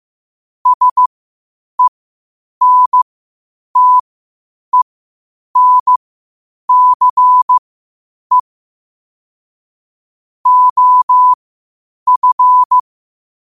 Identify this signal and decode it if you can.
Morse code, heard as a keyed tone, about 15 wpm, spaced as for 8 wpm, 1000 Hz. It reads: SENTENCE OF
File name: qotdfiles / 8wpm